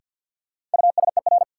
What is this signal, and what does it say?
Morse code, heard as a keyed tone, about 52 wpm, 700 Hz. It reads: USER
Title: Sentences from Top 1000 Words - Review Part 5 - New Word Drill - 50wpm